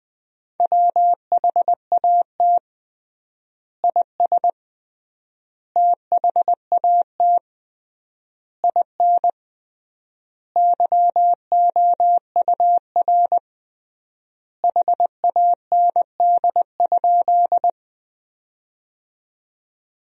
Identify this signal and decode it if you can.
Morse code, heard as a keyed tone, about 20 wpm, 700 Hz. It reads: WHAT IS THAT IN YOUR HAND?
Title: 3x Word-Spacing - Sentences from Top 200 Words - 20wpm